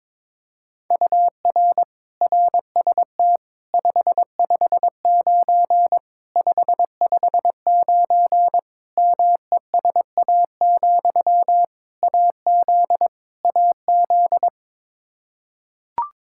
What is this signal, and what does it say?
Morse code, heard as a keyed tone, about 21 wpm, 700 Hz. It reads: UR RST 559 559 MESA, AZ AZ
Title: Practice QSO - Exchange 1 - Beginning - 22wpm